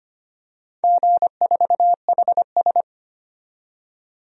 Morse code, heard as a keyed tone, about 25 wpm, 700 Hz. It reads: G4HS
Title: DXCC-100 Call Signs - 4 Characters - 25wpm